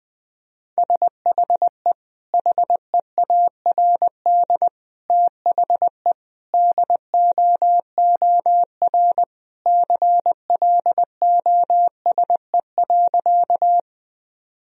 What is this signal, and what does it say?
Morse code, heard as a keyed tone, about 21 wpm, 700 Hz. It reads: SHE HEARD THE DOOR CLOSE.
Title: Sentences from Top 1000 Words - Review Part 1 - 20wpm